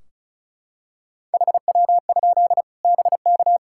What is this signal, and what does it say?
Morse code, heard as a keyed tone, about 36 wpm, 700 Hz. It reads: HW? BK
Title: Practice QSO - Exchange 1 - Beginning - 35wpm